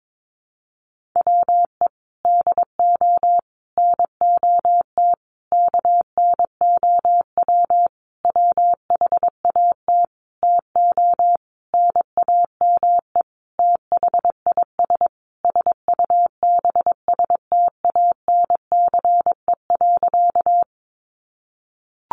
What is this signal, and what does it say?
Morse code, heard as a keyed tone, about 22 wpm, 700 Hz. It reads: WE DO NOT KNOW WHAT TO NAME THIS SUBSTANCE.
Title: Sentences from Top 1500 Words 22wpm